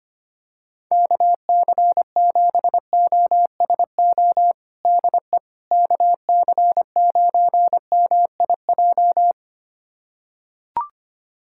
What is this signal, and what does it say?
Morse code, heard as a keyed tone, about 25 wpm, 700 Hz. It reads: KC7OSO DE KC9MIJ
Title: USA Callsign DE Callsign - 6 Characters - 25wpm